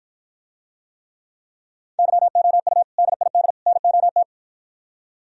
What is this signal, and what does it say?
Morse code, heard as a keyed tone, about 52 wpm, 700 Hz. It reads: YOU DID NOT